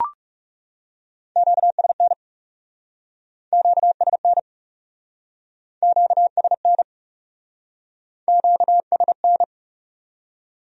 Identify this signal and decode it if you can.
Morse code, heard as a keyed tone, about 40 wpm, 700 Hz. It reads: QSN QSN QSN QSN